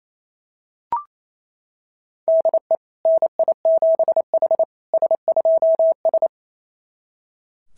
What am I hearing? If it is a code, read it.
Morse code, heard as a keyed tone, about 28 wpm, 690 Hz. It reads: DE NI7H S2S
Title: SOTA QSO Practice 28wpm